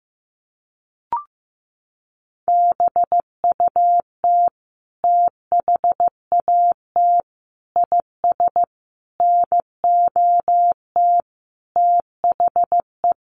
Morse code, heard as a keyed tone, about 15 wpm, 700 Hz. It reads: BUT THAT IS NOT THE
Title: Sentences from Top 600 Words 15wpm